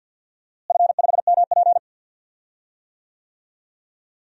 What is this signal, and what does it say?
Morse code, heard as a keyed tone, about 46 wpm, 700 Hz. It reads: F5MP
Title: DXCC-100 Call Signs - 4 Characters - Repeated 2x - 50wpm